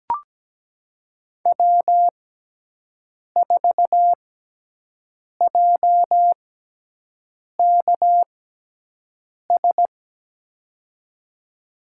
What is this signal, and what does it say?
Morse code, heard as a keyed tone, about 17 wpm, spaced as for 6 wpm, 700 Hz. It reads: W4JKS
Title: USA Call Signs - 5 Characters - 6x Character Spacing - 17wpm